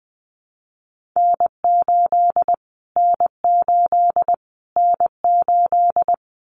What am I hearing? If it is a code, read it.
Morse code, heard as a keyed tone, about 20 wpm, 700 Hz. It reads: N8 N8 N8
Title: Callsign Prefixes for Top 25 DXCC - Repeated 3x - 20wpm